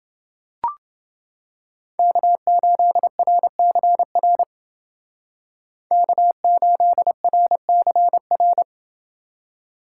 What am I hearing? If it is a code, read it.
Morse code, heard as a keyed tone, about 29 wpm, 700 Hz. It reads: K8RCR K8RCR